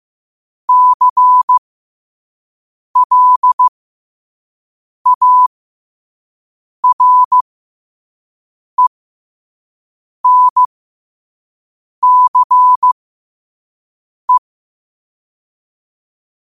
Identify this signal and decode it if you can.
Morse code, heard as a keyed tone, about 15 wpm, spaced as for 5 wpm, 1000 Hz. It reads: CLARENCE
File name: qotdfiles / 5wpm